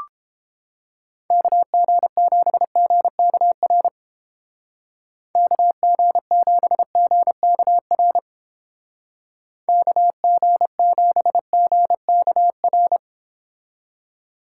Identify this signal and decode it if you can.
Morse code, heard as a keyed tone, about 31 wpm, 700 Hz. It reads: KG7GKR KG7GKR KG7GKR